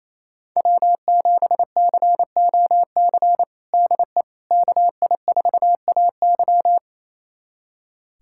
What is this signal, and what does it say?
Morse code, heard as a keyed tone, about 28 wpm, 700 Hz. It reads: W7COC DE KI4AY
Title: USA Callsign DE Callsign - 5 Characters - 28wpm